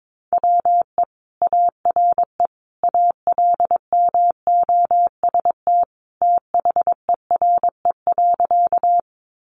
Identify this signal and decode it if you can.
Morse code, heard as a keyed tone, about 22 wpm, 700 Hz. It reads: WE ARE ALMOST THERE.
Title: Sentences from Top 700 Words 22wpm